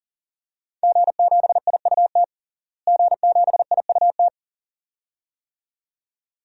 Morse code, heard as a keyed tone, about 40 wpm, 700 Hz. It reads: G7IUT G7IUT
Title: DXCC-100 Call Signs - 5 Characters - Repeated 2x - 40wpm